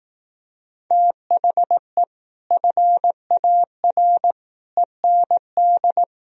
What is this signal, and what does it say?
Morse code, heard as a keyed tone, about 18 wpm, 700 Hz. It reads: THE FAR END
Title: Speed-Racing - Sets of 5 Words 15wpm